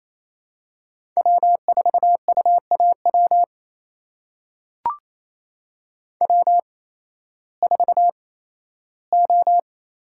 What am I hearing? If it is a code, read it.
Morse code, heard as a keyed tone, about 28 wpm, spaced as for 15 wpm, 700 Hz. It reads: W4UAW W4O
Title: USA Call Signs - 8x Character Spacing - 28wpm